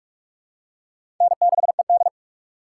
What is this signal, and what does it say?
Morse code, heard as a keyed tone, about 44 wpm, 700 Hz. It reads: N6ED
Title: USA Call Signs - 4 Characters - 6x Character Spacing - 45wpm